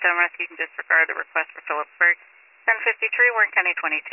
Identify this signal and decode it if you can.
Emergency Alert System (SAME) tones, heard as harsh buzzing bursts, with speech. It is heard from 0.7 to 3.6 s.